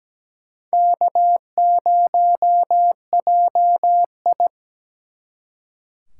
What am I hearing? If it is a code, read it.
Morse code, heard as a keyed tone, about 21 wpm, 700 Hz. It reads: K0JI